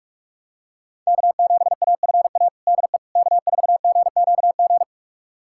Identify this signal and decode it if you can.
Morse code, heard as a keyed tone, about 46 wpm, 700 Hz. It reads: K7AFA DE K4GQG
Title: USA Callsign DE Callsign - 5 Characters - 45wpm